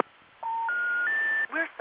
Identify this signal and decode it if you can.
Special information tone, with noise.